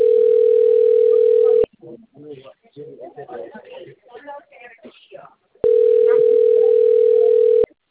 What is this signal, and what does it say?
Ringback tone (North America), 440 + 480 Hz, with speech.
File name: out- - - - - .42003